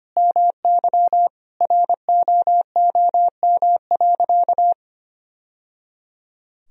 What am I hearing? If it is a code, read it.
Morse code, heard as a keyed tone, about 25 wpm, 700 Hz. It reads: MY ROOM.
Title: Sentences from Top 1000 Words - Review Part 1 - 25wpm